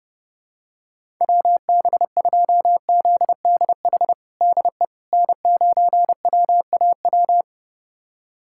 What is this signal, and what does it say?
Morse code, heard as a keyed tone, about 30 wpm, 700 Hz. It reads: WB2ZDH DE N9WAW
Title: USA Callsign DE Callsign 30wpm